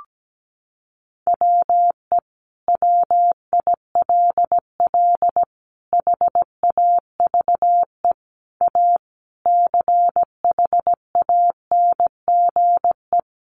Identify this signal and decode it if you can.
Morse code, heard as a keyed tone, about 17 wpm, 700 Hz. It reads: WE WILL HAVE A CHANGE
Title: Sentences from Top 1400 Words 17wpm